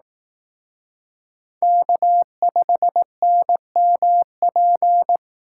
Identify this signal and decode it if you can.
Morse code, heard as a keyed tone, about 18 wpm, 700 Hz. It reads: K5NMP
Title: Speed Racing - USA 1x3 Call Signs 15wpm